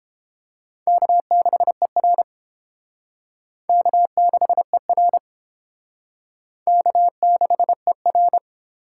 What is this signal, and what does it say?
Morse code, heard as a keyed tone, about 29 wpm, 700 Hz. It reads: K6ER K6ER K6ER